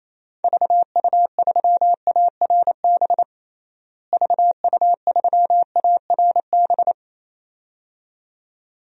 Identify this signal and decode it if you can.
Morse code, heard as a keyed tone, about 28 wpm, 700 Hz. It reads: VU3ARB VU3ARB
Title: DXCC-100 Call Signs - 6 Characters - Repeated 2x - 28wpm